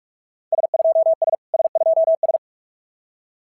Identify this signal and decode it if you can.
Morse code, heard as a keyed tone, about 44 wpm, 690 Hz. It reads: S2S S2S
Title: SOTA QSO Practice 45wpm